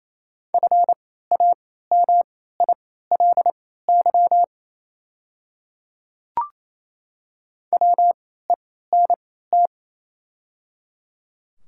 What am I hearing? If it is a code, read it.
Morse code, heard as a keyed tone, about 28 wpm, spaced as for 16 wpm, 700 Hz. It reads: FAMILY WENT
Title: 3x Character Spacing - Top 500 Words - 28wpm